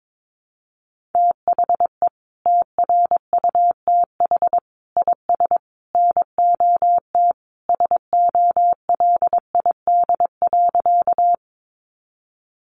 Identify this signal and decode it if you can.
Morse code, heard as a keyed tone, about 22 wpm, 700 Hz. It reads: THE TRUTH IS NOT SOLID.
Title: Sentences from Top 1500 Words 22wpm